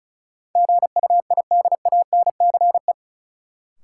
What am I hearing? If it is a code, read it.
Morse code, heard as a keyed tone, about 34 wpm, 700 Hz. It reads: GUIDANCE
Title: Sentences from Top 1600 Words - New Word Drill - 35wpm